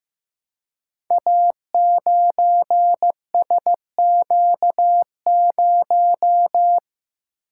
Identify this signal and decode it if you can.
Morse code, heard as a keyed tone, about 15 wpm, 700 Hz. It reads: A9SQ0
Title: Code Groups - 5 Letter:Numbers - 15wpm